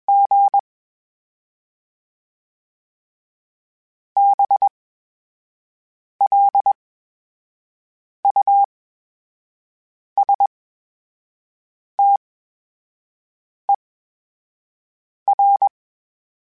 Morse code, heard as a keyed tone, about 21 wpm, spaced as for 5 wpm, 800 Hz. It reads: G BLUSTER
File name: Special INGs-mp3s / Special INGs-mp3s / VerySlowToFaster Ing @20